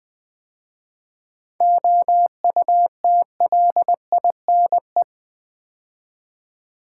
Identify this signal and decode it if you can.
Morse code, heard as a keyed tone, about 20 wpm, 700 Hz. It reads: OUTLINE
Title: 7 Letter Words - First 3 Letters Given - 20wpm